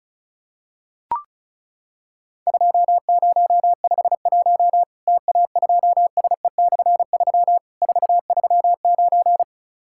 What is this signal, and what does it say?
Morse code, heard as a keyed tone, about 36 wpm, 700 Hz. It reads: 2051 TA2SE/3 439
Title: Worked All Europe DX Contest - QTC Exchange - 35wpm